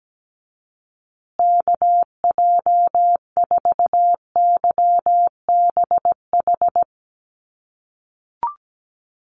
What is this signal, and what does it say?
Morse code, heard as a keyed tone, about 17 wpm, 700 Hz. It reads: KJ4YBH
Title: USA Call Signs Repeated 2x 17wpm